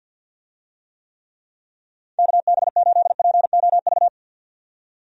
Morse code, heard as a keyed tone, about 48 wpm, 700 Hz. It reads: KB8POV
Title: USA Call Signs - 6 Characters - 50wpm